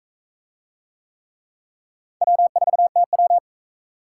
Morse code, heard as a keyed tone, about 44 wpm, 700 Hz. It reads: W4TW